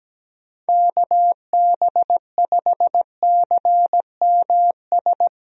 Morse code, heard as a keyed tone, about 17 wpm, 700 Hz. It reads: KB5CMS
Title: USA Call Signs - 6 Characters - Repeated 3x - 17wpm